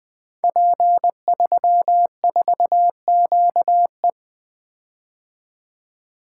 Morse code, heard as a keyed tone, about 20 wpm, 700 Hz. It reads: P34QE
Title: Code Groups - 5 Letter:Numbers - 20wpm